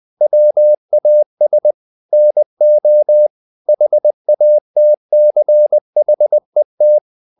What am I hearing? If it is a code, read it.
Morse code, heard as a keyed tone, about 20 wpm, 600 Hz. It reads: WAS NO HATCHET
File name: the old man and the sea part 2 20wpm